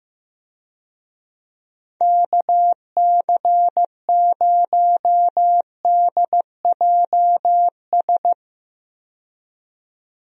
Morse code, heard as a keyed tone, about 16 wpm, 700 Hz. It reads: KC0DJS
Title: USA Call Signs - 6 Characters - 15wpm